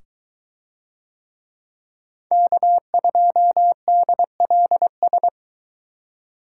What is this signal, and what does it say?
Morse code, heard as a keyed tone, about 23 wpm, 700 Hz. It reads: K2DLS